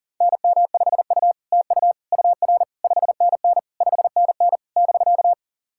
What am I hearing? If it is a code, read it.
Morse code, heard as a keyed tone, about 43 wpm, 700 Hz. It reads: NM5U TU UR 5NN 5NN <BK>
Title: SOTA QSO Practice 40wpm